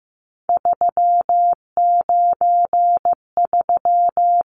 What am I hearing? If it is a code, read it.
Morse code, heard as a keyed tone, about 15 wpm, 700 Hz. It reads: 393
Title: Numbers - 3-Digit 15wpm